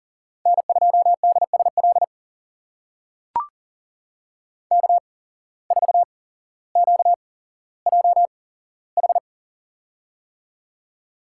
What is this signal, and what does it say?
Morse code, heard as a keyed tone, about 40 wpm, spaced as for 17 wpm, 700 Hz. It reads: N2DSL K4QJH
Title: USA Call Signs - 5 Characters - 8x Character Spacing - 40wpm